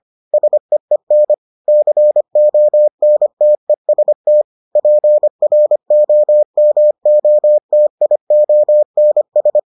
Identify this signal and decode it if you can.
Morse code, heard as a keyed tone, about 25 wpm, 600 Hz. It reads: SEEN CONTEST PROMOTIONS